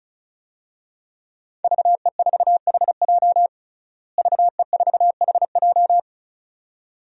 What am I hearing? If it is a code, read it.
Morse code, heard as a keyed tone, about 34 wpm, 700 Hz. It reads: VE4HJ VE4HJ
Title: DXCC-100 Call Signs - 5 Characters - Repeated 2x - 35wpm